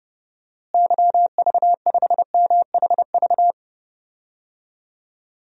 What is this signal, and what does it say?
Morse code, heard as a keyed tone, about 32 wpm, 700 Hz. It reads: YV5MHV
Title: DXCC-100 Call Signs - 6 Characters - Repeated 3x - 30wpm